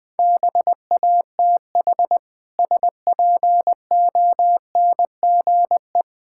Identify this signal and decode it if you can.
Morse code, heard as a keyed tone, about 20 wpm, 700 Hz. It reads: BATH SPONGE
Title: Difficult Words - Speed-Racing with Hints - 20wpm